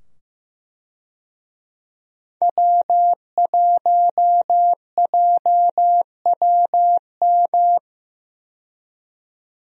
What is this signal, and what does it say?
Morse code, heard as a keyed tone, about 15 wpm, 700 Hz. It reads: W1JWM
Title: USA Call Signs - 5 Characters - 15wpm